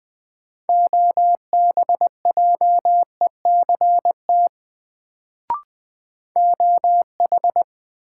Morse code, heard as a keyed tone, about 20 wpm, 700 Hz. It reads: OBJECT OH
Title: Sentences from Top 500 Words - New Word Drill - 20wpm